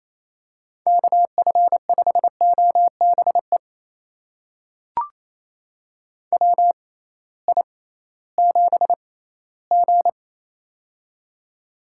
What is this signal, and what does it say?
Morse code, heard as a keyed tone, about 28 wpm, spaced as for 16 wpm, 700 Hz. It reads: KF5OBE WI7G
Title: USA Call Signs - 6x Character Spacing - 28wpm